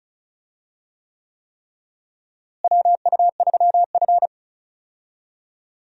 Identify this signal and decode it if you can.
Morse code, heard as a keyed tone, about 36 wpm, 700 Hz. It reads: WU3F
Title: USA Call Signs - 4 Characters - Repeated 3x - 35wpm